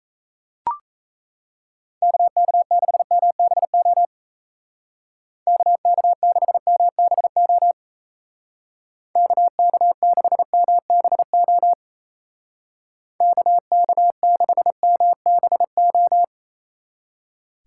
Morse code, heard as a keyed tone, about 34 wpm, 700 Hz. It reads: KK6MBO KK6MBO KK6MBO KK6MBO
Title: Speed Racing - USA 2x3 Call Signs 28wpm